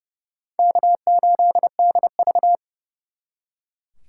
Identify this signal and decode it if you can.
Morse code, heard as a keyed tone, about 39 wpm, 700 Hz. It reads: K8DV